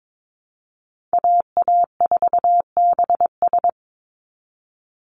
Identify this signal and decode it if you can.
Morse code, heard as a keyed tone, about 22 wpm, 700 Hz. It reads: AA4BS
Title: USA Call Signs 22wpm